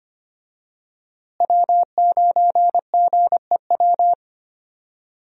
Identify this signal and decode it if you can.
Morse code, heard as a keyed tone, about 25 wpm, 700 Hz. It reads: W9GEW
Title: USA Call Signs - 5 Characters - Repeated 2x - 25wpm